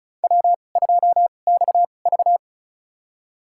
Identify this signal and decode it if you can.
Morse code, heard as a keyed tone, about 34 wpm, spaced as for 25 wpm, 700 Hz. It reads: W2XV
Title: USA Call Signs - 2x Character Spacing - 35wpm